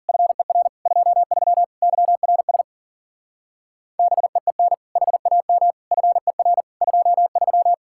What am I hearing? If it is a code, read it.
Morse code, heard as a keyed tone, about 49 wpm, 700 Hz. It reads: FER 23 YRS BEEN HAM FER 23
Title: Speed-Racing Common QSO Phrases - 40wpm